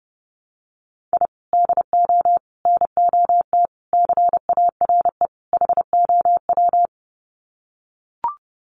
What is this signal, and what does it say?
Morse code, heard as a keyed tone, about 30 wpm, 700 Hz. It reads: I DO NOT CARE HOW
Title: Sets of 5 Words Encore2 30wpm